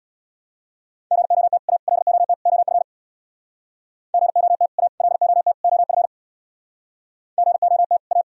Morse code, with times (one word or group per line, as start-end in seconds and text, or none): 1.1-1.6 GOT
1.7-1.8 A
1.9-2.3 LOT
2.5-2.8 OF
4.1-4.7 GOT
4.8-4.9 A
5.0-5.5 LOT
5.6-6.1 OF
7.4-8.0 GOT
8.1-8.2 A